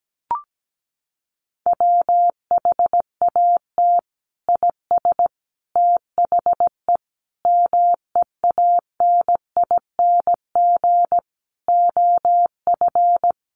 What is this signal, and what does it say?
Morse code, heard as a keyed tone, about 17 wpm, 700 Hz. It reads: WHAT IS THE MEANING OF
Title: Sentences from Top 900 Words 17wpm